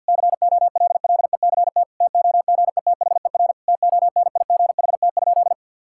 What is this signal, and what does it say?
Morse code, heard as a keyed tone, about 47 wpm, 700 Hz. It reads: COLLECT TOGETHER TONIGHT?
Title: Sentences from Top 1600 Words 50wpm